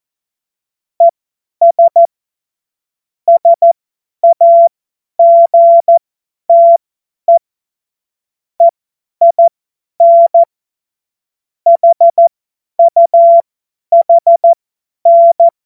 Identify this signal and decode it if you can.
Morse code, heard as a keyed tone, about 14 wpm, spaced as for 10 wpm, 680 Hz. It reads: ES SAGTE EIN HUHN
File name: Morsekurs / Witze / Witz 20. Juni 2023 50 F70